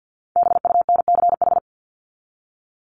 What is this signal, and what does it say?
Morse code, heard as a keyed tone, about 48 wpm, 700 Hz. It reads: BUNCH
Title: Top 1751-2000 Words 50wpm